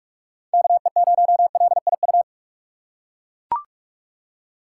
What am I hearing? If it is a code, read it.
Morse code, heard as a keyed tone, about 42 wpm, 700 Hz. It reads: KE0LIU
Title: USA Call Signs - 3x Character Spacing - 45wpm